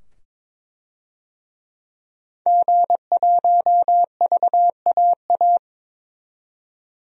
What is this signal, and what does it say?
Morse code, heard as a keyed tone, about 22 wpm, 700 Hz. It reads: G1VAA
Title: DXCC-100 Call Signs - 5 Characters - 22wpm